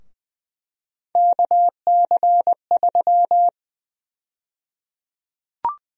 Morse code, heard as a keyed tone, about 21 wpm, 700 Hz. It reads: KC3
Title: USA Call Sign Prefixes - Repeated 2x - 20wpm